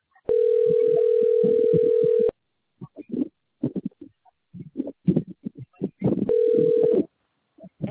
Ringback tone (North America), 440 + 480 Hz.